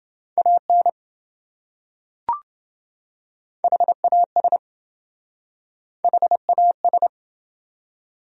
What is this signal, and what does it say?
Morse code, heard as a keyed tone, about 27 wpm, 700 Hz. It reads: AN HAS HAS